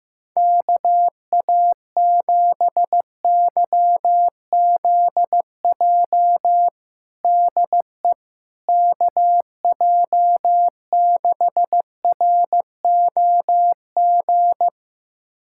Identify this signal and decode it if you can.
Morse code, heard as a keyed tone, about 15 wpm, 700 Hz. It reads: KA7YZJ DE KJ6ROG
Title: USA Callsign DE Callsign - 6 Characters - 15wpm